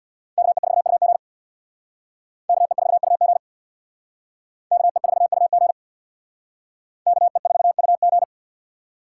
Morse code, heard as a keyed tone, about 61 wpm, 700 Hz. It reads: KE4UG KE4UG KE4UG KE4UG
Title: Speed Racing - USA 2x2 Call Signs 50wpm